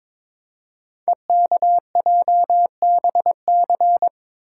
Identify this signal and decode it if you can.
Morse code, heard as a keyed tone, about 22 wpm, 700 Hz. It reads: EKJBC
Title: Code Groups - 5 Letters - 22wpm